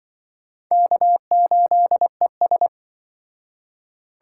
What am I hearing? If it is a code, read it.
Morse code, heard as a keyed tone, about 24 wpm, 700 Hz. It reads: K8ES
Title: Speed Racing - USA Call Signs 20wpm